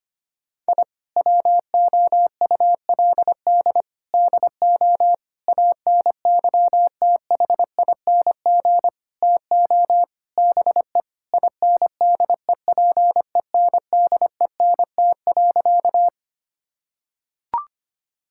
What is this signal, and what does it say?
Morse code, heard as a keyed tone, about 25 wpm, 700 Hz. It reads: I WOULD DO ANYTHING TO BE INDEPENDENT.
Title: Sentences from Top 1000 Words - Review Part 6 - 25wpm